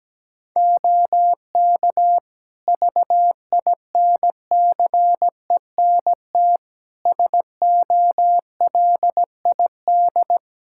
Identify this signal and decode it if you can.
Morse code, heard as a keyed tone, about 17 wpm, 700 Hz. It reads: OK VINCENT SOLID
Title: Practice QSO - Exchange 3 - More Info - 17wpm